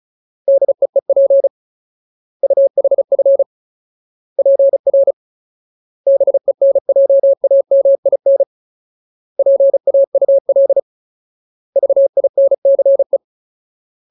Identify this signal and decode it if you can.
Morse code, heard as a keyed tone, about 34 wpm, 550 Hz. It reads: DEEP UHF PR BENJAMIN PAUL VINCE
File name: Ultimate Word List / 35WPM Ultimate